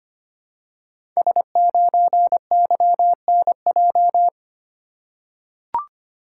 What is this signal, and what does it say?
Morse code, heard as a keyed tone, about 25 wpm, 700 Hz. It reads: S9YNJ
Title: Code Groups - 5 Letter:Numbers - 25wpm